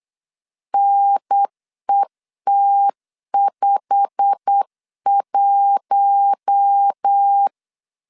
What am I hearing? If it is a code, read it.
Morse code, heard as a keyed tone, about 8 wpm, 800 Hz. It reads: NET51
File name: LEZIONI / 8 YJ45 / 8 YJ45 8wpm 3214